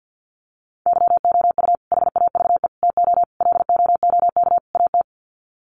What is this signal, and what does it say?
Morse code, heard as a keyed tone, about 51 wpm, 700 Hz. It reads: YOU HAVE TO LOOK AT
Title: Sets of 5 Words Encore 50wpm